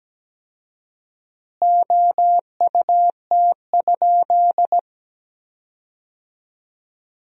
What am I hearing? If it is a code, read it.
Morse code, heard as a keyed tone, about 17 wpm, 700 Hz. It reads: OUT?
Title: 4x Word-Spacing - Sentences from Top 200 Words - 17wpm